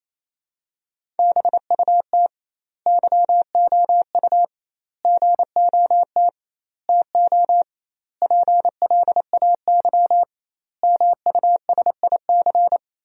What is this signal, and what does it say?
Morse code, heard as a keyed tone, about 27 wpm, 700 Hz. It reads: BUT YOU GOT TO PLAY MUSIC
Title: 2x Word-Spacing - Sentences from Top 300 Words - 28wpm